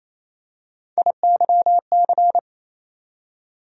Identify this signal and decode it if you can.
Morse code, heard as a keyed tone, about 28 wpm, 700 Hz. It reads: IYC
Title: USA Call Sign Postfixes - 3 Letters 28wpm